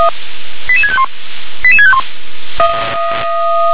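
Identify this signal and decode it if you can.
Five-tone selective call (ZVEI), heard as quick runs of five tones, tones of about 70 ms. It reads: 88651 88651
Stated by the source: ZVEI five-tone call received by SDR.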